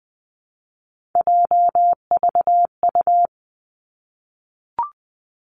Morse code, heard as a keyed tone, about 20 wpm, 700 Hz. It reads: JVU
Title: Three Challenging Letter-Number Combinations 20wpm